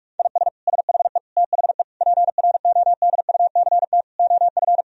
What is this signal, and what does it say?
Morse code, heard as a keyed tone, about 46 wpm, 700 Hz. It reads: IS SHE THE PRODUCT OF
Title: Sentences from Top 1000 Words - Review Part 2 - 45wpm